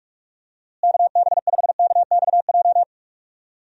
Morse code, heard as a keyed tone, about 41 wpm, 700 Hz. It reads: KB5KXJ
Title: USA Call Signs - 6 Characters - Repeated 3x - 45wpm